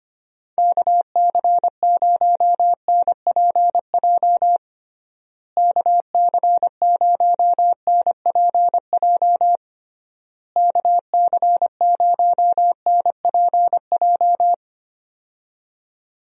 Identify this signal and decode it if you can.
Morse code, heard as a keyed tone, about 25 wpm, 700 Hz. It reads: KC0NPJ KC0NPJ KC0NPJ
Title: USA Call Signs - 6 Characters - Repeated 3x - 25wpm